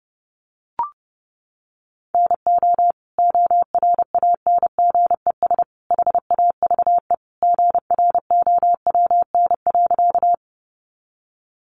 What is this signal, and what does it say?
Morse code, heard as a keyed tone, about 30 wpm, 700 Hz. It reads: NO ORANGES HAVE GROWN.
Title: Easy Phrases 30wpm